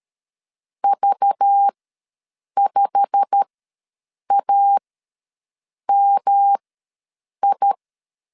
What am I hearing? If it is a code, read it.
Morse code, heard as a keyed tone, about 13 wpm, spaced as for 7 wpm, 800 Hz. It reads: V5AMI
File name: LEZIONI / 10 SIGNS / 10D COMA ws 12wpm 3618